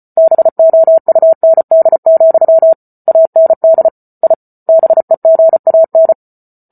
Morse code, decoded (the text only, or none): BOUND, AND I BEGAN